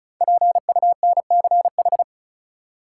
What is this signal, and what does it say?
Morse code, heard as a keyed tone, about 34 wpm, 700 Hz. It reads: PUNCH
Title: Five Letter Words - 35wpm